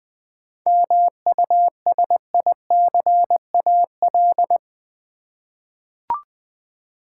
Morse code, heard as a keyed tone, about 20 wpm, 700 Hz. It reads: MUSICAL
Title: 7 Letter Words - First 3 Letters Given - 20wpm